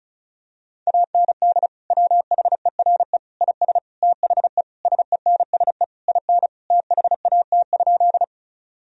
Morse code, heard as a keyed tone, about 36 wpm, 700 Hz. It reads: AND WHERE IS THE SENSE IN THAT?
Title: Sentences from Top 1000 Words - Review Part 4 - 35wpm